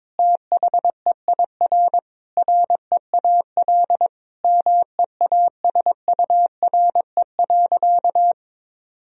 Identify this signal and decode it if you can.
Morse code, heard as a keyed tone, about 22 wpm, 700 Hz. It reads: THEIR REAL MEASURE.
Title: Sentences from Top 1000 Words - Review Part 1 - 22wpm